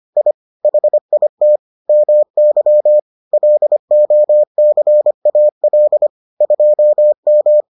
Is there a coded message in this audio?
Yes: Morse code from 0.2 to 7.6 s, heard as a keyed tone, about 25 wpm, 600 Hz. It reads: I HIT MY LOCAL 2M